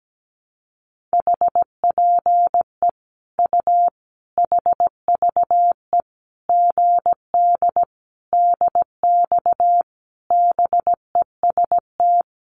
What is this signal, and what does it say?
Morse code, heard as a keyed tone, about 17 wpm, 700 Hz. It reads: HPE U HVE GD DX BEST
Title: Common QSO Phrases 17wpm